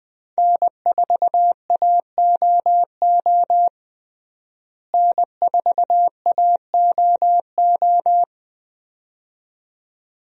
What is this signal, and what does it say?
Morse code, heard as a keyed tone, about 20 wpm, 700 Hz. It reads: N4AOO N4AOO
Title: USA Call Signs - 5 Characters - Repeated 2x - 20wpm